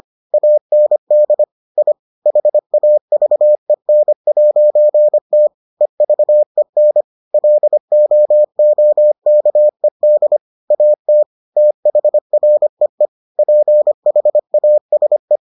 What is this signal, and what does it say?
Morse code, heard as a keyed tone, about 25 wpm, 600 Hz. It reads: AND I HAVEN'T EVEN LOOKED AT THREE PHASE